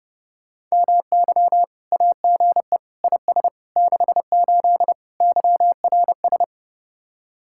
Morse code, heard as a keyed tone, about 37 wpm, 700 Hz. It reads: MY AGE IS 68 YRS
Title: Speed-Racing Common QSO Phrases - 30wpm